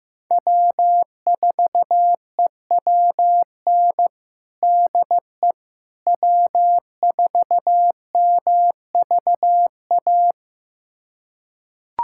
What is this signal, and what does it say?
Morse code, heard as a keyed tone, about 16 wpm, 700 Hz. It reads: W4EWN DE W4MVA
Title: USA Callsign DE Callsign - 5 Characters - 15wpm